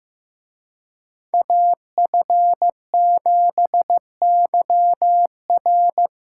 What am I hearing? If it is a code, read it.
Morse code, heard as a keyed tone, about 18 wpm, 700 Hz. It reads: AF7YR
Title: Speed Racing - USA Call Signs 15wpm